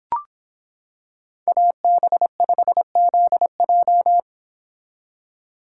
Morse code, heard as a keyed tone, about 25 wpm, 700 Hz. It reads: AB5ZJ